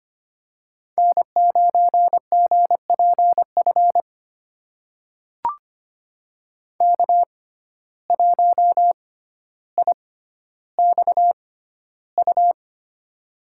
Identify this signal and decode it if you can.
Morse code, heard as a keyed tone, about 25 wpm, spaced as for 13 wpm, 700 Hz. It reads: N9GPF K1IXU
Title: USA Call Signs - 5 Characters - 6x Character Spacing - 25wpm